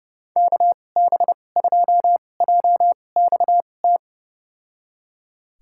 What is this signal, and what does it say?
Morse code, heard as a keyed tone, about 30 wpm, spaced as for 22 wpm, 700 Hz. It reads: KB2JXT
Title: USA Call Signs - 2x Character Spacing - 30wpm